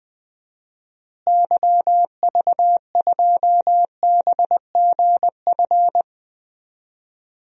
Morse code, heard as a keyed tone, about 21 wpm, 700 Hz. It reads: YV2BGF
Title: DXCC-100 Call Signs - 6 Characters - Repeated 2x - 20wpm